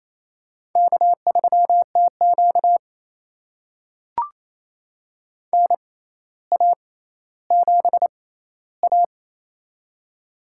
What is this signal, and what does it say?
Morse code, heard as a keyed tone, about 28 wpm, spaced as for 14 wpm, 700 Hz. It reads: K3TQ NA7A
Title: USA Call Signs - 4 Characters - 6x Character Spacing - 28wpm